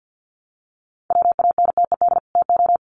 Morse code, heard as a keyed tone, about 50 wpm, 700 Hz. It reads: WANTED TO
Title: Sets of 2 Words 50wpm